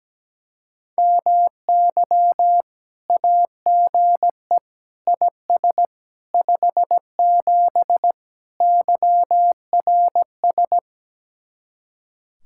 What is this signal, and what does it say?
Morse code, heard as a keyed tone, about 17 wpm, 700 Hz. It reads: MY AGE IS 57 YRS
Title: Speed-Racing Common QSO Phrases Encore - 17wpm